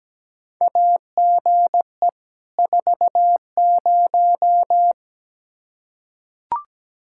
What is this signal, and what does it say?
Morse code, heard as a keyed tone, about 17 wpm, 700 Hz. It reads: AGE 40
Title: Practice QSO - Exchange 3 - More Info - 17wpm